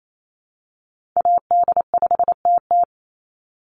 Morse code, heard as a keyed tone, about 28 wpm, 700 Hz. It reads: AD5TT
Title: Call Signs - 28wpm